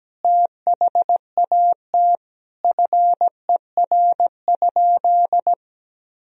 Morse code, heard as a keyed tone, about 17 wpm, 700 Hz. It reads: THAT FER?
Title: Sentences from Top 200 Words and Abbreviations 17wpm